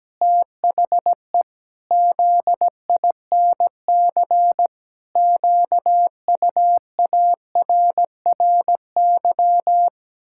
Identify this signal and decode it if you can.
Morse code, heard as a keyed tone, about 17 wpm, 700 Hz. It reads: THE ZINC QUARRY
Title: Pangram Challenge v2 - 17wpm